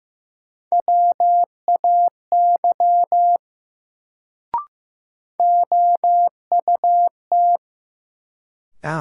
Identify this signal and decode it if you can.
Morse code, heard as a keyed tone, about 15 wpm, 700 Hz. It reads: WAY OUT
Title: Sentences from Top 100 Words - New Word Drill - 15wpm